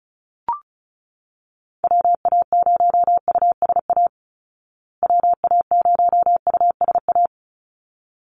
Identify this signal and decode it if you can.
Morse code, heard as a keyed tone, about 34 wpm, 700 Hz. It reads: WA0USA WA0USA
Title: Call Signs Repeated 3x - 35wpm